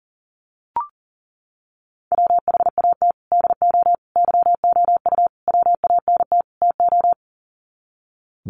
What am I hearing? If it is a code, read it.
Morse code, heard as a keyed tone, about 40 wpm, 700 Hz. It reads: WHAT DO YOU WANT TO
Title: Sets of 5 Words 40wpm